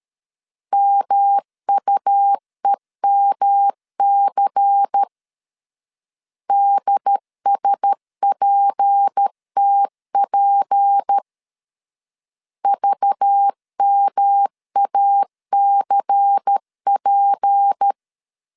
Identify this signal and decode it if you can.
Morse code, heard as a keyed tone, about 13 wpm, 800 Hz. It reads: MUEMC DSPTP VMACP
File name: LEZIONI / 6 PKX0 / 6 P 12wpm 2033